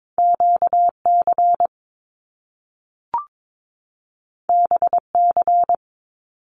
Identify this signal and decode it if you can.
Morse code, heard as a keyed tone, about 22 wpm, 700 Hz. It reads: QC BC
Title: Canadian Provinces and Territories 22wpm